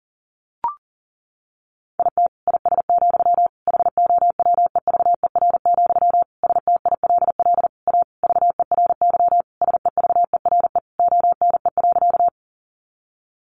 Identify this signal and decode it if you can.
Morse code, heard as a keyed tone, about 40 wpm, 700 Hz. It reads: IT IS, HOWEVER, STILL A VERY SEVERE ONE.
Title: Sentences from Top 1400 Words 40wpm